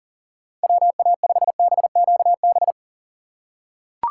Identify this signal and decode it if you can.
Morse code, heard as a keyed tone, about 39 wpm, 700 Hz. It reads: WA5BQB
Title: USA Call Signs - 6 Characters - 2x Character Spacing - 40wpm